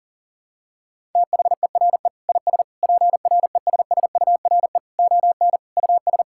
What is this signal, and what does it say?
Morse code, heard as a keyed tone, about 40 wpm, 700 Hz. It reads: THERE IS PRESSURE ON US
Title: Sentences from Top 1000 Words - Review Part 6 - 40wpm